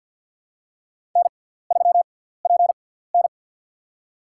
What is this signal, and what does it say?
Morse code, heard as a keyed tone, about 49 wpm, spaced as for 17 wpm, 700 Hz. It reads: N3PN
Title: USA Call Signs - 4 Characters - 6x Character Spacing - 50wpm